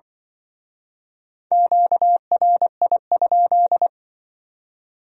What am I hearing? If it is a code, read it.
Morse code, heard as a keyed tone, about 25 wpm, 700 Hz. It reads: QRI?
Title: Speed-Racing Q-Codes - 20wpm